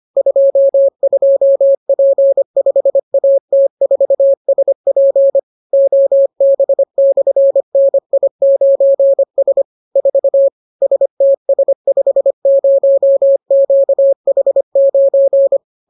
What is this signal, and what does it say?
Morse code, heard as a keyed tone, about 25 wpm, 550 Hz. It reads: 22P5AT4SP OB/NI9S 4 STS50QH9